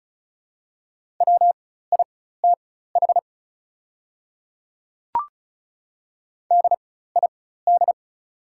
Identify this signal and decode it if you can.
Morse code, heard as a keyed tone, about 36 wpm, spaced as for 16 wpm, 700 Hz. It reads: WITH DID
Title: 4x Character Spacing - Top 100 Words - 35wpm